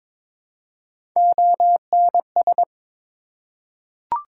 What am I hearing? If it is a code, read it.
Morse code, heard as a keyed tone, about 22 wpm, 700 Hz. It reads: ONS
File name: ARRL Sections v2 - 22wpm